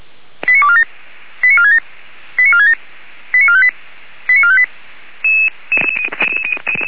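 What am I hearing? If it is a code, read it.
Five-tone selective call (ZVEI), heard as quick runs of five tones, tones of about 70 ms. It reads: 89247 89467 89467 89458 89458
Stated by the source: Five-tone ZVEI selective call received by SDR, from the Feuerwehr Orlamünde (f85360) audio archive.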